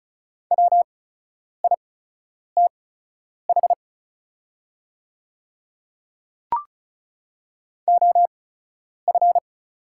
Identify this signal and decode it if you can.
Morse code, heard as a keyed tone, about 36 wpm, spaced as for 10 wpm, 690 Hz. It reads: WITH OF
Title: 8x Character Spacing - Top 200 Words - 35wpm